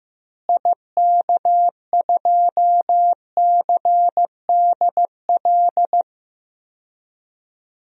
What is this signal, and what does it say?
Morse code, heard as a keyed tone, about 15 wpm, 700 Hz. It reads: IK2CDL
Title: DXCC-100 Call Signs - 6 Characters - Repeated 3x - 15wpm